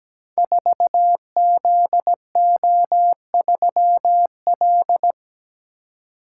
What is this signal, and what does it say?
Morse code, heard as a keyed tone, about 17 wpm, 700 Hz. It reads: 4ZO3L
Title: Code Groups - 5 Letter:Numbers - 17wpm